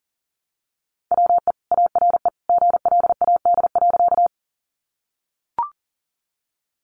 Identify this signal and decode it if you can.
Morse code, heard as a keyed tone, about 40 wpm, 700 Hz. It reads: WE ARE GLAD.
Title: Sentences from Top 800 Words 40wpm